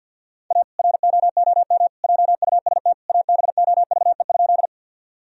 Morse code, heard as a keyed tone, about 47 wpm, 700 Hz. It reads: A ROOM JUST ABOVE?
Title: Sentences from Top 1000 Words - Review Part 2 - 50wpm